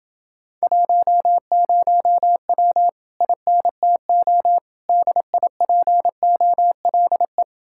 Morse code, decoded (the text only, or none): 10W INTO DIPOLE